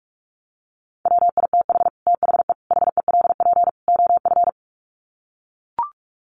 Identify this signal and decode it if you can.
Morse code, heard as a keyed tone, about 44 wpm, 700 Hz. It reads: WITH THE HELP OF
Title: Sets of 4 Words 45wpm